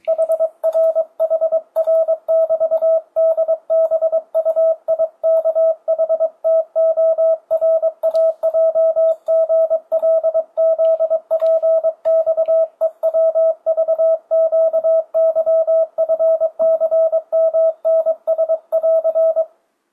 Morse code, heard as a keyed tone, about 22 wpm, 650 Hz, with radio noise. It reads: HR HR <BT> DBUIKHTORAJGLZPXEWVQYFCMNS <AR>
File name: 2025cw / speedup / 23wpm